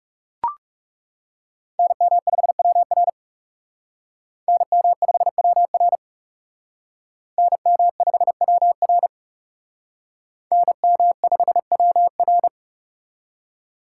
Morse code, heard as a keyed tone, about 37 wpm, 700 Hz. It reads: NM5WR NM5WR NM5WR NM5WR
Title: Speed Racing - USA Call Signs 30wpm